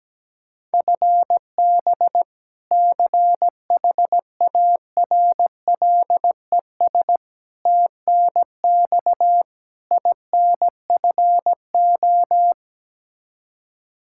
Morse code, heard as a keyed tone, about 17 wpm, 700 Hz. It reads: FB CHARLES TNX INFO
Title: Practice QSO - Exchange 3 - More Info - 17wpm